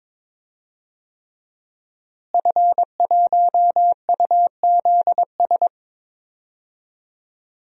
Morse code, heard as a keyed tone, about 22 wpm, 700 Hz. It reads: F1UZS
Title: DXCC-100 Call Signs - 5 Characters - Repeated 3x - 22wpm